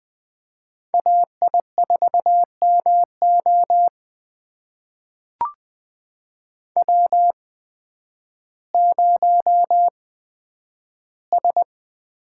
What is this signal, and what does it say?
Morse code, heard as a keyed tone, about 20 wpm, spaced as for 11 wpm, 700 Hz. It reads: AI4MO W0S
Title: USA Call Signs - 5 Characters - 8x Character Spacing - 20wpm